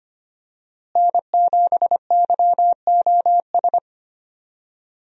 Morse code, heard as a keyed tone, about 25 wpm, 700 Hz. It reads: N7YOS
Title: USA Call Signs - 5 Characters - Repeated 2x - 25wpm